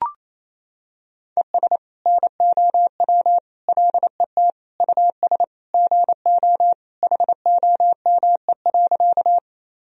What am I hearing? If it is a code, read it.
Morse code, heard as a keyed tone, about 28 wpm, 700 Hz. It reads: ES NOW LET US GO HOME.
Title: Sentences from Top 300 Words and Abbreviations 28wpm